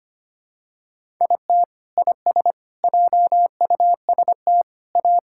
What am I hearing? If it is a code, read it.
Morse code, heard as a keyed tone, about 25 wpm, 700 Hz. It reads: IT IS JUST A
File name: practice / 25wpm / Sentences with 2-Word Phrases - 25wpm